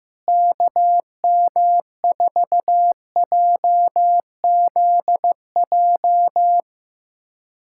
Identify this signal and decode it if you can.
Morse code, heard as a keyed tone, about 16 wpm, 700 Hz. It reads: KM4JZJ
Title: USA Callsign DE Callsign - 6 Characters - 15wpm